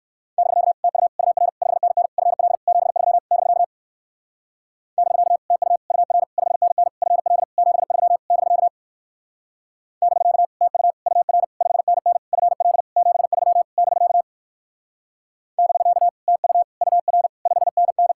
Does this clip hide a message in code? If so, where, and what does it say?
Yes: Morse code from 0.4 to 18.1 s, heard as a keyed tone, about 57 wpm, 700 Hz. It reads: <BK> TU UR 5NN FL 73 <BK> <BK> TU UR 5NN FL 73 <BK> <BK> TU UR 5NN FL 73 <BK> <BK> TU UR 5NN